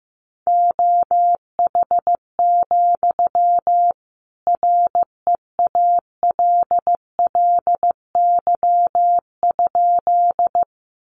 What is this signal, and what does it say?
Morse code, heard as a keyed tone, about 15 wpm, 700 Hz. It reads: OH, REALLY?
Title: Sentences from Top 600 Words 15wpm